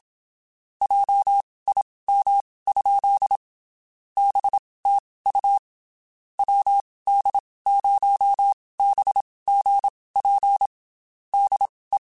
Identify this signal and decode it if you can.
Morse code, heard as a keyed tone, about 27 wpm, spaced as for 19 wpm, 800 Hz. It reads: JIM? BTU WD0BGP DE
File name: QSOs / QSOs-mp3s / WD0FBT 1@25